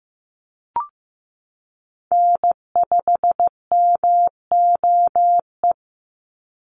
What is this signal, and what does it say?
Morse code, heard as a keyed tone, about 15 wpm, 700 Hz. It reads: N5MOE
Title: USA Call Signs 15wpm